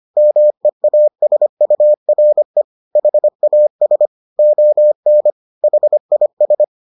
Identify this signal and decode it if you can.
Morse code, heard as a keyed tone, about 25 wpm, 600 Hz. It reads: MEASURE HAS ON HIS